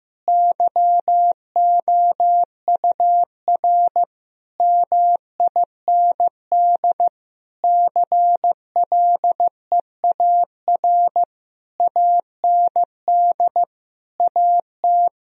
Morse code, heard as a keyed tone, about 15 wpm, 700 Hz. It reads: YOUR MIND CLEAR AND AT
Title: Sentences from Top 1000 Words - Review Part 1 - 15wpm